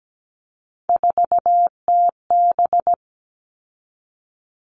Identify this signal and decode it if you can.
Morse code, heard as a keyed tone, about 17 wpm, 700 Hz. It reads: 4TB
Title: All Cut Numbers - 3-Digit 17wpm